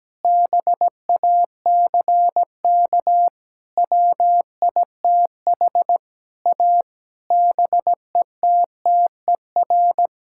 Morse code, heard as a keyed tone, about 17 wpm, 700 Hz. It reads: BACK WITH A BETTER
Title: Sentences from Top 1000 Words - Review Part 6 - 17wpm